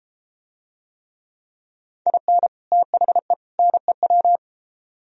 Morse code, heard as a keyed tone, about 32 wpm, 700 Hz. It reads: IN THE NEW